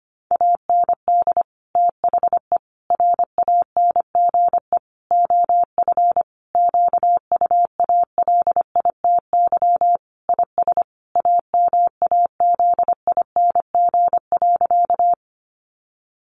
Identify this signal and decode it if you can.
Morse code, heard as a keyed tone, about 25 wpm, 700 Hz. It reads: AND THE RANGE OF QUALITY IS AMAZING.
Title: Sentences from Top 1500 Words 25wpm